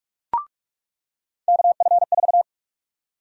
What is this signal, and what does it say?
Morse code, heard as a keyed tone, about 48 wpm, 700 Hz. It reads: KF4
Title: USA Call Sign Prefixes - 45wpm